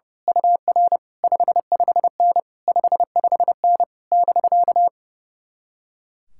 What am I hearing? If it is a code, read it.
Morse code, heard as a keyed tone, about 28 wpm, 700 Hz. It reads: UR 55N 55N <BK>
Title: Speed-Racing Common QSO Phrases - 30wpm